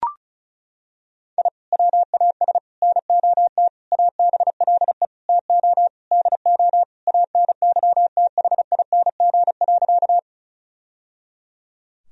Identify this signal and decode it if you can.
Morse code, heard as a keyed tone, about 36 wpm, 700 Hz. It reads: I WAS NOT ABLE TO DO ANYTHING.
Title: Sentences from Top 1000 Words - Review Part 2 - 35wpm